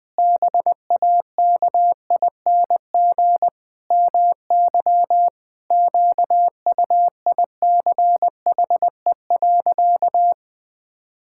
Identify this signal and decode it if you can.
Morse code, heard as a keyed tone, about 21 wpm, 700 Hz. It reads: BAKING MY QUICHE.
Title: Pangram Challenge v2 - 20wpm